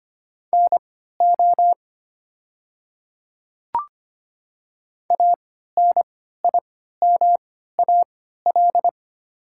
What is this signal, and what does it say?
Morse code, heard as a keyed tone, about 25 wpm, spaced as for 14 wpm, 700 Hz. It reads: NO ANIMAL
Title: 3x Character Spacing - Top 200 Words - 25wpm